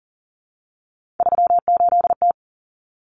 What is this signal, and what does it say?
Morse code, heard as a keyed tone, about 40 wpm, 700 Hz. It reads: 38T
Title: Common Cut Numbers - 3-Digit 40wpm